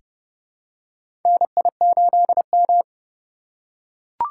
Morse code, heard as a keyed tone, about 31 wpm, 700 Hz. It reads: NI8M
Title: USA Call Signs - 4 Characters - 3x Character Spacing - 30wpm